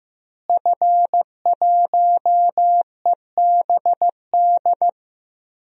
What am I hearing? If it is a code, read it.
Morse code, heard as a keyed tone, about 15 wpm, 700 Hz. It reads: F1EBD
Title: DXCC-100 Call Signs - 5 Characters - Repeated 3x - 15wpm